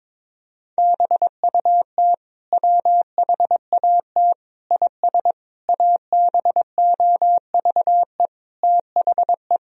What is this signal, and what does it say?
Morse code, heard as a keyed tone, about 22 wpm, 700 Hz. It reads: BUT WHAT IS ABOVE THE
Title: Sentences from Top 1000 Words - Review Part 2 - 22wpm